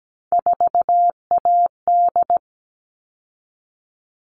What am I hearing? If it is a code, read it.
Morse code, heard as a keyed tone, about 17 wpm, 700 Hz. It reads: 4AD
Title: All Cut Numbers - 3-Digit 17wpm